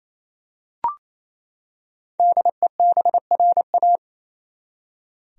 Morse code, heard as a keyed tone, about 28 wpm, 700 Hz. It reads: DEBRA
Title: USA Common Names - Max 5 Letters - 28wpm